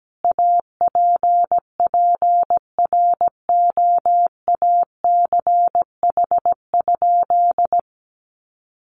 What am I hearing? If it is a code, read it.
Morse code, heard as a keyed tone, about 17 wpm, 700 Hz. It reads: APPROACH?
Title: Sentences from Top 1500 Words 17wpm